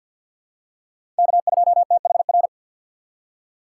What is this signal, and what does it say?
Morse code, heard as a keyed tone, about 52 wpm, 700 Hz. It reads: K2THR
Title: USA Call Signs - 5 Characters - 2x Character Spacing - 50wpm